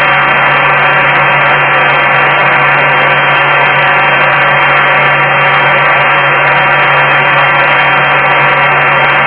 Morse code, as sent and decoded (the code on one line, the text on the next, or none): none